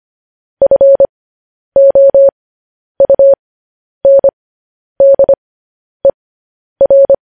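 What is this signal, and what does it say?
Morse code, heard as a keyed tone, about 25 wpm, spaced as for 10 wpm, 550 Hz. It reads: FOUNDER
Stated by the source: Morse code news headlines at 10 wpm.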